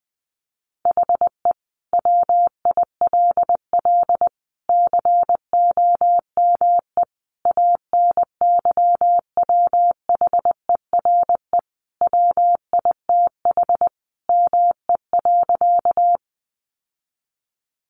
Morse code, heard as a keyed tone, about 20 wpm, 700 Hz. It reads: HE WILL COME ANYWHERE WITH ME.
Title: Sentences from Top 1400 Words 20wpm